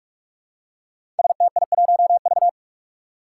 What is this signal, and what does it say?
Morse code, heard as a keyed tone, about 46 wpm, 700 Hz. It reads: STI1V
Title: Five Challenging Letter-Number Combinations 45wpm